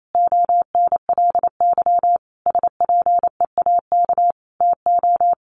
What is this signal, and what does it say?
Morse code, heard as a keyed tone, about 28 wpm, 700 Hz. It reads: ONLY SPEAK TO
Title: Sentences from Top 1100 Words 28wpm